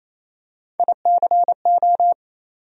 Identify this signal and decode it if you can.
Morse code, heard as a keyed tone, about 28 wpm, 700 Hz. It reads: ICO
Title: USA Call Sign Postfixes - 3 Letters 28wpm